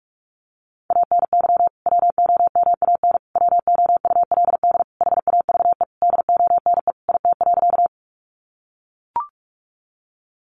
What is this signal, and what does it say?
Morse code, heard as a keyed tone, about 44 wpm, 700 Hz. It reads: ANY WOMAN WOULD HAVE DONE IT.
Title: Sentences from Top 1100 Words 45wpm